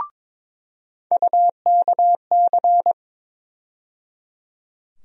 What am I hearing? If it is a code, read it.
Morse code, heard as a keyed tone, about 22 wpm, 700 Hz. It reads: UKC
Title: USA Call Sign Postfixes - 3 Letters 22wpm